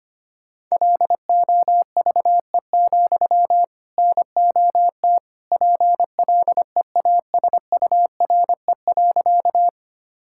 Morse code, heard as a keyed tone, about 25 wpm, 700 Hz. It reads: LOVE, NOT PLEASURE.
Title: Sentences from Top 1600 Words 25wpm